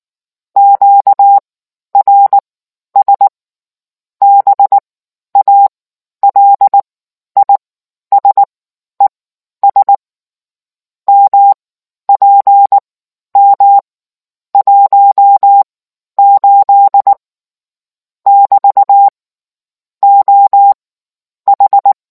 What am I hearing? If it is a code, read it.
Morse code, heard as a keyed tone, about 19 wpm, spaced as for 11 wpm, 800 Hz. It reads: QRS BALISES MPM 18 <BT> OH